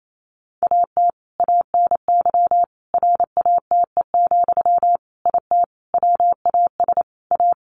Morse code, heard as a keyed tone, about 28 wpm, 700 Hz. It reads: AT ANY RATE, IT WAS A
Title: Sentences from Top 1200 Words 28wpm